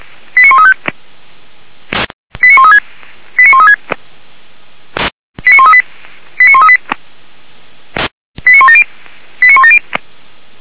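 Five-tone selective call (ZVEI), heard as quick runs of five tones, tones of about 70 ms. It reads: 89135 89136 89136 89138 89138 89170 89170
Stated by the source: ZVEI five-tone call received on SDR.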